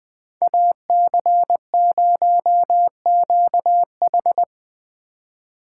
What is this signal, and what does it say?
Morse code, heard as a keyed tone, about 20 wpm, 700 Hz. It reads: AC0QH
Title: Speed Racing - USA 2x2 Call Signs 17wpm